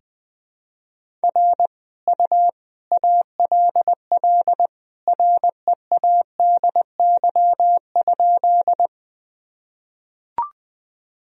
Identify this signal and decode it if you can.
Morse code, heard as a keyed tone, about 20 wpm, 700 Hz. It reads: R U ALL READY?
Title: Sentences from Top 400 Words and Abbreviations 20wpm